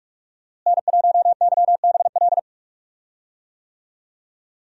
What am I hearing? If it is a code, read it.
Morse code, heard as a keyed tone, about 43 wpm, 700 Hz. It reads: N1YBL
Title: USA Call Signs - 5 Characters - Repeated 2x - 45wpm